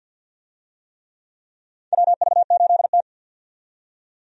Morse code, heard as a keyed tone, about 48 wpm, 700 Hz. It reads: WV8T
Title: USA Call Signs - 4 Characters - 50wpm